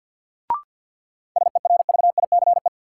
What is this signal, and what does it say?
Morse code, heard as a keyed tone, about 47 wpm, 700 Hz. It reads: SERVICE
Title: Sentences from Top 700 Words - New Word Drill - 50wpm